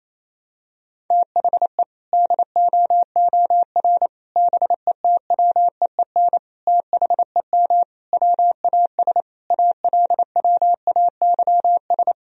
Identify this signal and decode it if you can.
Morse code, heard as a keyed tone, about 28 wpm, 700 Hz. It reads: THE DOOR BETWEEN THEM WAS ALWAYS
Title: Sentences from Top 1000 Words - Review Part 1 - 28wpm